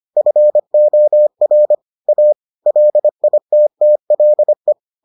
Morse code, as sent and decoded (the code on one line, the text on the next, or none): ..-. --- .-. / .- / .-.. .. - - .-.. .
FOR A LITTLE